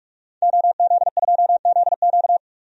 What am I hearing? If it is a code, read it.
Morse code, heard as a keyed tone, about 46 wpm, 700 Hz. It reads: OZ2ZQ
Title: DXCC-100 Call Signs - 5 Characters - 45wpm